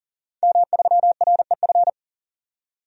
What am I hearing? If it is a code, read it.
Morse code, heard as a keyed tone, about 40 wpm, 700 Hz. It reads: M3REF
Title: DXCC-100 Call Signs - 5 Characters - 40wpm